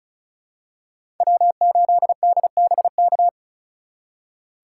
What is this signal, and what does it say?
Morse code, heard as a keyed tone, about 36 wpm, 690 Hz. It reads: W8DBK
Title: USA Call Signs - 5 Characters - 3x Character Spacing - 35wpm